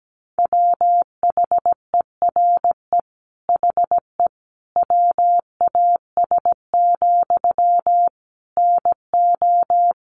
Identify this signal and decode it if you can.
Morse code, heard as a keyed tone, about 17 wpm, 700 Hz. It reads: WHERE HE WAS, NO
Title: Sentences from Top 400 Words 17wpm